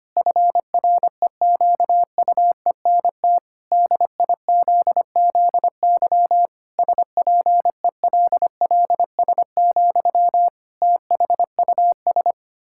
Morse code, heard as a keyed tone, about 25 wpm, 700 Hz. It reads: FREQUENT DIZZY SPELLS, THUS
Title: Pangram Challenge v2 - 25wpm